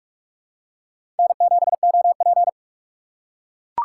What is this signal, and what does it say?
Morse code, heard as a keyed tone, about 46 wpm, 700 Hz. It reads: N7OP
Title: USA Call Signs - 4x Character Spacing - 45wpm